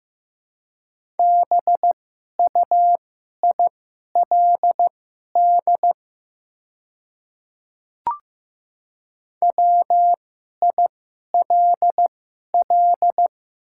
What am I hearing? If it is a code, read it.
Morse code, heard as a keyed tone, about 15 wpm, spaced as for 11 wpm, 700 Hz. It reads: BUILD WILL
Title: 2x Character Spacing - Top 500 Words - 15wpm